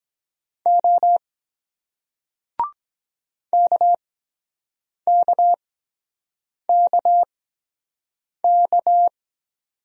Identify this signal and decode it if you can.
Morse code, heard as a keyed tone, about 23 wpm, spaced as for 7 wpm, 700 Hz. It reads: O KKKK